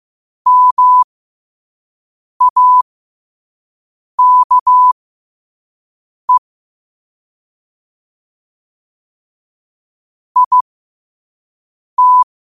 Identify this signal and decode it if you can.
Morse code, heard as a keyed tone, about 15 wpm, spaced as for 5 wpm, 1000 Hz. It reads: MAKE IT
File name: qotdfiles / 5wpm